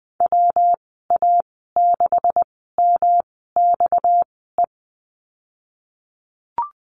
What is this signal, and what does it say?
Morse code, heard as a keyed tone, about 20 wpm, spaced as for 15 wpm, 700 Hz. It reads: WA6MXE
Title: 238 Lesson 38g - Letter q - Call Signs 20wpm